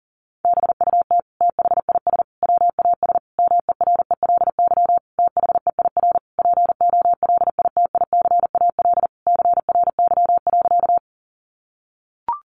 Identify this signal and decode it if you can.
Morse code, heard as a keyed tone, about 37 wpm, 700 Hz. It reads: BUT THIS WAS MERELY THEIR POLITICAL CRY.
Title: Sentences from Top 1400 Words 40wpm